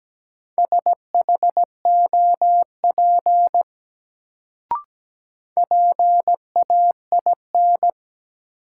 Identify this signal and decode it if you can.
Morse code, heard as a keyed tone, about 17 wpm, 700 Hz. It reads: SHOP PAIN
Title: Sentences from Top 1600 Words - New Word Drill - 17wpm